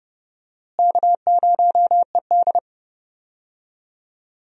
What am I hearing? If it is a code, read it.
Morse code, heard as a keyed tone, about 31 wpm, 700 Hz. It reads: K0ED
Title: USA Call Signs - 4 Characters - Repeated 3x - 30wpm